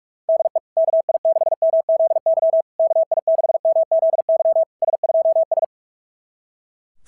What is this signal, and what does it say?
Morse code, heard as a keyed tone, about 44 wpm, 660 Hz. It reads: DE KI6MZY KI6MZY S2S
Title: SOTA QSO Practice 45wpm